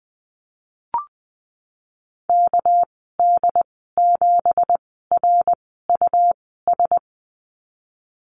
Morse code, heard as a keyed tone, about 20 wpm, spaced as for 15 wpm, 700 Hz. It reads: KD7RUS
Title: 207 Lesson 33g - Letter j - Call Signs 20wpm